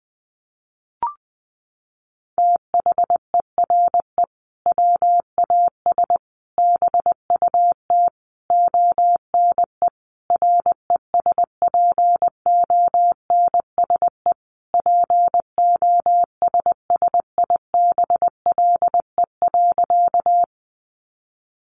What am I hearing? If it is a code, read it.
Morse code, heard as a keyed tone, about 20 wpm, 700 Hz. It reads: THERE WAS BUT ONE RESPONSE POSSIBLE.
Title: Sentences from Top 1400 Words 20wpm